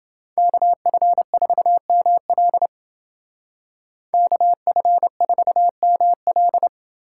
Morse code, heard as a keyed tone, about 28 wpm, 700 Hz. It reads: KF4ML KF4ML